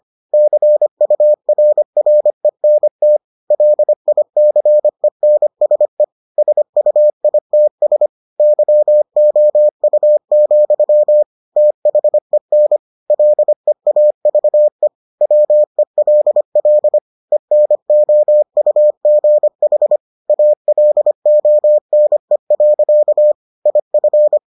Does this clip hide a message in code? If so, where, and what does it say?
Yes: Morse code from 0.3 to 24.4 s, heard as a keyed tone, about 25 wpm, 600 Hz. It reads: CURRENT LICENSE SUITS YOU, THEN LEAVE WELL ENOUGH ALONE. IF